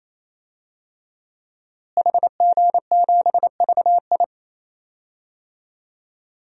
Morse code, heard as a keyed tone, about 28 wpm, 700 Hz. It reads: HG7VI
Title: DXCC-100 Call Signs - 5 Characters - Repeated 3x - 28wpm